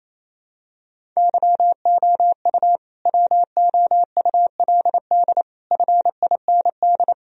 Morse code, heard as a keyed tone, about 28 wpm, 700 Hz. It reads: YOU WOULD FIND
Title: Sentences from Top 1000 Words - Review Part 1 - 28wpm